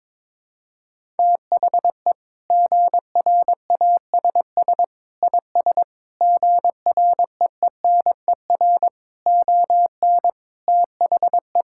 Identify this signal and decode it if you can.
Morse code, heard as a keyed tone, about 22 wpm, 700 Hz. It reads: THE GRASS IS GREENER ON THE
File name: practice / 22wpm / Sentences with 4-Word Phrases - 22wpm